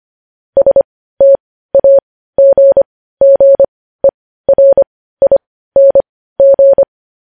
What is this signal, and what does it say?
Morse code, heard as a keyed tone, about 25 wpm, spaced as for 15 wpm, 550 Hz. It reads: STAGGERING
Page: news-headlines-in-morse-code-at-15-wpm